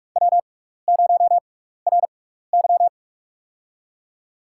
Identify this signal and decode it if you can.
Morse code, heard as a keyed tone, about 44 wpm, spaced as for 15 wpm, 700 Hz. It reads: W0RY
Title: USA Call Signs - 4 Characters - 6x Character Spacing - 45wpm